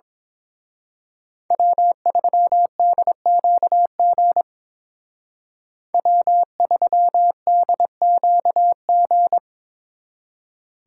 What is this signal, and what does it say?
Morse code, heard as a keyed tone, about 26 wpm, 700 Hz. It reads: W3DQG W3DQG
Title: Speed Racing - USA 1x3 Call Signs 22wpm